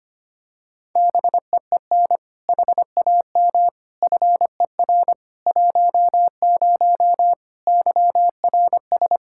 Morse code, heard as a keyed tone, about 25 wpm, 700 Hz. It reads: BEEN HAM FER 10 YRS
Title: Practice QSO - Exchange 3 - More Info - 25wpm